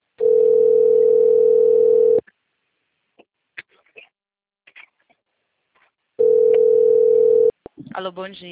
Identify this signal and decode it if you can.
Ringback tone (North America), 440 + 480 Hz, with speech.
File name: rg-1600-unknown- - - .10848